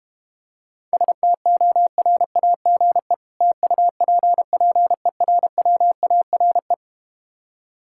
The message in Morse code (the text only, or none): STORAGE TUPPERWARE